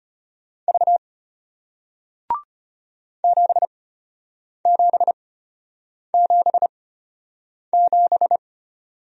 Morse code, heard as a keyed tone, about 33 wpm, spaced as for 8 wpm, 700 Hz. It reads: V 7777